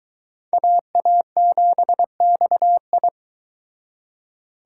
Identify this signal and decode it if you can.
Morse code, heard as a keyed tone, about 24 wpm, 700 Hz. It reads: AA7XI